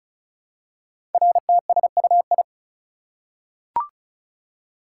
Morse code, heard as a keyed tone, about 36 wpm, 700 Hz. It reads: RTSUI
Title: Code Groups - 5 Letters - 35wpm